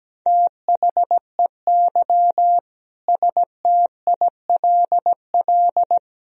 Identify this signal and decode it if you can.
Morse code, heard as a keyed tone, about 17 wpm, 700 Hz. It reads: THEY STILL
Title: Sentences from Top 1000 Words - Review Part 4 - 17wpm